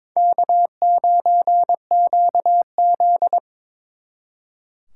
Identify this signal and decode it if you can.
Morse code, heard as a keyed tone, about 26 wpm, 700 Hz. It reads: K9QZ
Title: Speed Racing - USA 1x2 Call Signs 22wpm